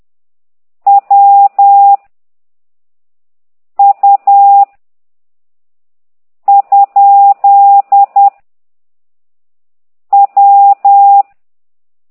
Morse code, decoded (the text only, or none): W U ? W